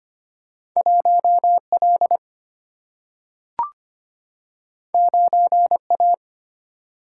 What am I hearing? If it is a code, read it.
Morse code, heard as a keyed tone, about 25 wpm, 700 Hz. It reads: 1 L 9 A
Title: Single-Letter Single-Number 25wpm